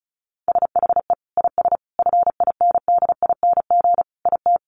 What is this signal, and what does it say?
Morse code, heard as a keyed tone, about 34 wpm, 700 Hz. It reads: SHE IS FINDING IT
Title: Sentences from Top 1100 Words 35wpm